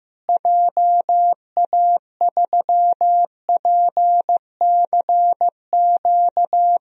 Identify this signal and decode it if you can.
Morse code, heard as a keyed tone, about 15 wpm, 700 Hz. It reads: JA3PCQ
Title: DXCC-100 Call Signs - 6 Characters - Repeated 3x - 15wpm